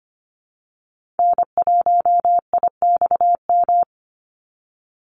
Morse code, heard as a keyed tone, about 25 wpm, 700 Hz. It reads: N1IXM
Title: USA Call Signs 25wpm